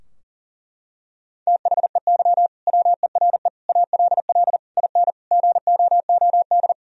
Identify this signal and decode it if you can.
Morse code, heard as a keyed tone, about 40 wpm, 700 Hz. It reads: THEY WERE ALL IN GOOD
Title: Sentences from Top 1000 Words - Review Part 3 - 40wpm